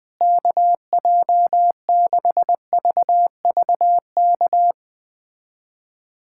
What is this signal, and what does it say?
Morse code, heard as a keyed tone, about 20 wpm, 700 Hz. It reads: KJ6VVK
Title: USA Callsign DE Callsign - 6 Characters - 20wpm